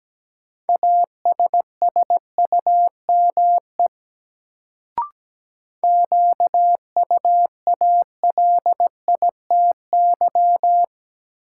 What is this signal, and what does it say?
Morse code, heard as a keyed tone, about 17 wpm, 700 Hz. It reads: ASSUME QUALITY
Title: Sentences from Top 1500 Words - New Word Drill Alt - 17wpm